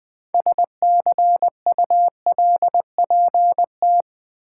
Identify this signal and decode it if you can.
Morse code, heard as a keyed tone, about 21 wpm, 700 Hz. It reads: SCULPT
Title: Build A Word - 3 Letters At A Time - 9-Letter Words - 20wpm